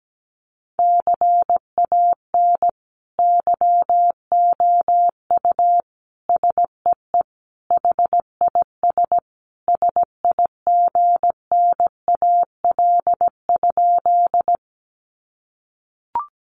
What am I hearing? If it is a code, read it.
Morse code, heard as a keyed tone, about 17 wpm, 700 Hz. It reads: CAN YOU SEE HIS SIGNAL?
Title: Sentences from Top 1400 Words 17wpm